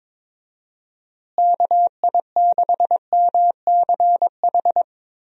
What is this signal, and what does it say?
Morse code, heard as a keyed tone, about 22 wpm, 700 Hz. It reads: KI6MCH
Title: USA Call Signs - 6 Characters - Repeated 2x - 22wpm